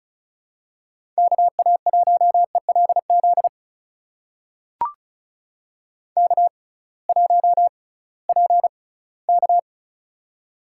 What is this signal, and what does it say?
Morse code, heard as a keyed tone, about 36 wpm, spaced as for 20 wpm, 700 Hz. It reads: KA1ELZ K1PK
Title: USA Call Signs - 6x Character Spacing - 35wpm